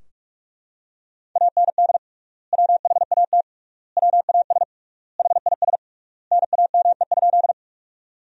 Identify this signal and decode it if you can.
Morse code, heard as a keyed tone, about 41 wpm, 700 Hz. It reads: AND WHAT WAS HIS NAME?
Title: 3x Word-Spacing - Sentences from Top 200 Words - 45wpm